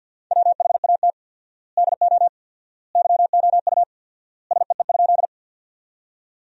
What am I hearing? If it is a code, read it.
Morse code, heard as a keyed tone, about 50 wpm, 700 Hz. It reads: WHAT DO YOU SEE?
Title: 4x Word-Spacing - Sentences from Top 100 Words - 50wpm